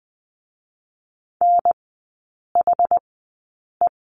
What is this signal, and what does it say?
Morse code, heard as a keyed tone, about 20 wpm, spaced as for 8 wpm, 690 Hz. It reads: NHE
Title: 060 Lesson 11e - Letter h - Three Characters 20wpm